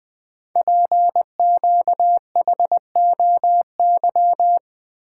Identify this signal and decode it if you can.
Morse code, heard as a keyed tone, about 20 wpm, 700 Hz. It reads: PQHOY
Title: Code Groups - 5 Letters - 20wpm